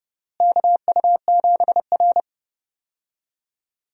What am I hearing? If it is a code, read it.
Morse code, heard as a keyed tone, about 30 wpm, 700 Hz. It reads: KU7R
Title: USA Call Signs - 4 Characters - Repeated 2x - 30wpm